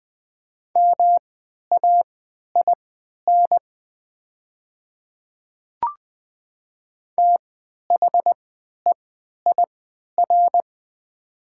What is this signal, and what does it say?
Morse code, heard as a keyed tone, about 20 wpm, spaced as for 11 wpm, 700 Hz. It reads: MAIN THEIR
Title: 3x Character Spacing - Top 300 Words - 20wpm